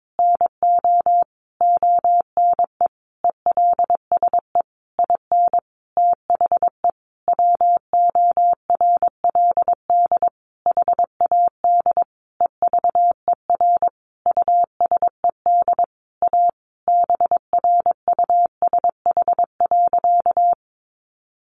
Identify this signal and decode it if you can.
Morse code, heard as a keyed tone, about 22 wpm, 700 Hz. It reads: NO ONE ELSE IN THE WORLD HAD EVER USED A BRUSH.
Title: Sentences from Top 1100 Words 22wpm